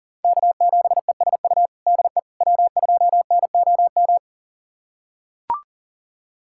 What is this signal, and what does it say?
Morse code, heard as a keyed tone, about 40 wpm, 700 Hz. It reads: K7ESU DE W2NOM
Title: USA Callsign DE Callsign - 5 Characters - 40wpm